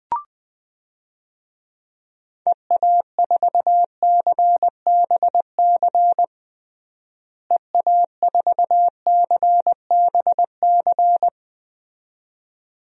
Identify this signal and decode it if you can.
Morse code, heard as a keyed tone, about 21 wpm, 700 Hz. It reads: EA4CBC EA4CBC
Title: DXCC-100 Call Signs - 6 Characters - Repeated 2x - 20wpm